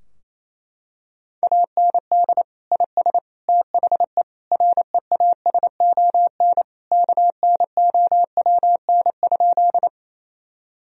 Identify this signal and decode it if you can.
Morse code, heard as a keyed tone, about 28 wpm, 700 Hz. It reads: AND IS THE REASON KNOWN?
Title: Sentences from Top 1000 Words - Review Part 3 - 28wpm